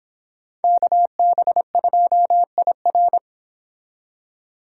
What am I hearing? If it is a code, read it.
Morse code, heard as a keyed tone, about 29 wpm, 700 Hz. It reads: KB2IR